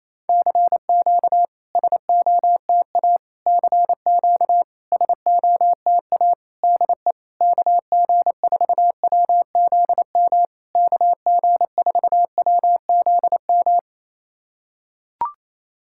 Morse code, heard as a keyed tone, about 28 wpm, 700 Hz. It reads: CQ SOTA CQ SOTA DE KG4WZM KG4WZM
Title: SOTA QSO Practice 28wpm